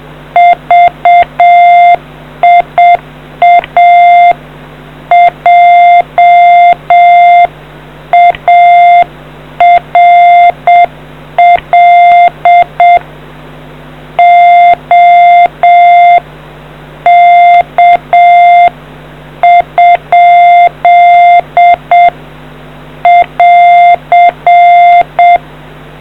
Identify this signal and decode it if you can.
Morse code, heard as a keyed tone, about 7 wpm, 700 Hz. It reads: VIA JARL OK ? <AR>